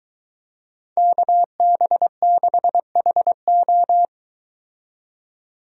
Morse code, heard as a keyed tone, about 22 wpm, 700 Hz. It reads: KB6HO